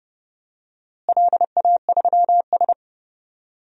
Morse code, heard as a keyed tone, about 31 wpm, 700 Hz. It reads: LA3S
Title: DXCC-100 Call Signs - 4 Characters - Repeated 2x - 30wpm